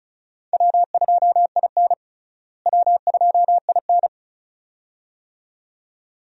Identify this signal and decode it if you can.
Morse code, heard as a keyed tone, about 34 wpm, 700 Hz. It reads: W2IN W2IN
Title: USA Call Signs - 4 Characters - Repeated 2x - 35wpm